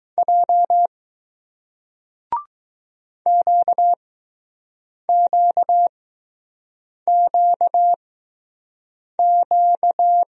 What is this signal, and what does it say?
Morse code, heard as a keyed tone, about 20 wpm, spaced as for 6 wpm, 700 Hz. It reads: J QQQQ